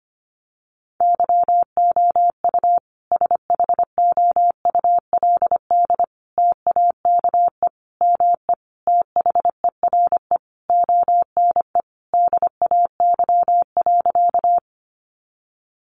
Morse code, heard as a keyed tone, about 25 wpm, 700 Hz. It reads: YOU SHOULD TAKE ME THERE ONE DAY.
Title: Sentences from Top 300 Words 25wpm